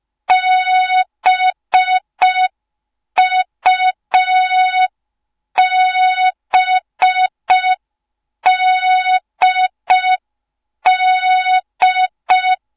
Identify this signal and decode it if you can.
Morse code, heard as a keyed tone, about 5 wpm, 3700 Hz. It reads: BUBDD